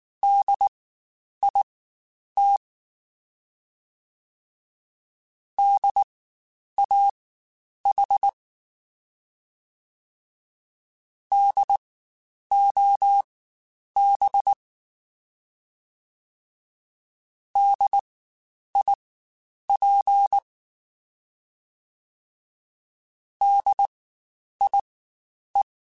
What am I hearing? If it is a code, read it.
Morse code, heard as a keyed tone, about 19 wpm, spaced as for 9 wpm, 800 Hz. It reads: DIT DAH DOB DIP DIE
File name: Spaced Alphas / Spaced Alphas mp3s / Ds sp @18